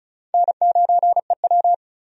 Morse code, heard as a keyed tone, about 36 wpm, 700 Hz. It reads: N9EW
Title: USA Call Signs - 4 Characters - 35wpm